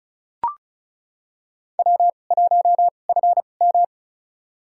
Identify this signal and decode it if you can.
Morse code, heard as a keyed tone, about 36 wpm, spaced as for 25 wpm, 700 Hz. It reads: W1FM
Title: USA Call Signs - 2x Character Spacing - 35wpm